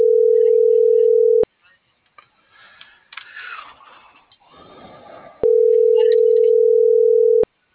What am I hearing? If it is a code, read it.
Ringback tone (North America), 440 + 480 Hz.